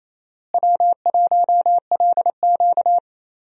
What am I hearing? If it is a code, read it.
Morse code, heard as a keyed tone, about 28 wpm, 700 Hz. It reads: W1LQ
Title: USA Call Signs - 4 Characters - Repeated 3x - 28wpm